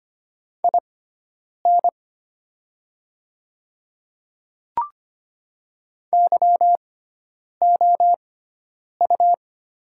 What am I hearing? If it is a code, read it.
Morse code, heard as a keyed tone, about 25 wpm, spaced as for 9 wpm, 700 Hz. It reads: IN YOU
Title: 6x Character Spacing - Top 100 Words - 25wpm